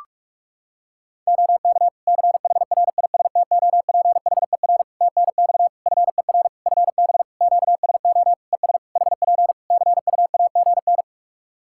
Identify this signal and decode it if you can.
Morse code, heard as a keyed tone, about 46 wpm, 700 Hz. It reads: OK CHRISTOPHER TNX FER FB QSO ES HP CUAGN
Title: Practice QSO - Exchange 4 - Ending - 45wpm